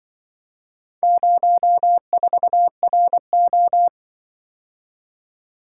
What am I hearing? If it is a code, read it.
Morse code, heard as a keyed tone, about 25 wpm, 700 Hz. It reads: 04RO